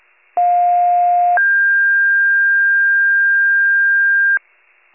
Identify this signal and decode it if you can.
Two-tone page, heard as a one-second tone, then a three-second tone, about 710 then 1600 Hz. It is heard from 0.4 to 4.4 s.